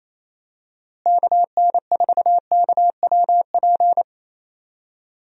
Morse code, heard as a keyed tone, about 28 wpm, 700 Hz. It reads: KN4KWP
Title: USA Call Signs - 6 Characters - Repeated 2x - 28wpm